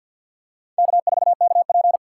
Morse code, heard as a keyed tone, about 47 wpm, 700 Hz. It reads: K4KP
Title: USA Call Signs - 4 Characters - 3x Character Spacing - 50wpm